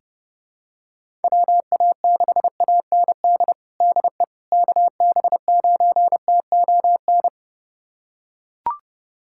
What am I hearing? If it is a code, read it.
Morse code, heard as a keyed tone, about 30 wpm, 700 Hz. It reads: WA6AND DE KB9TON
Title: USA Callsign DE Callsign 30wpm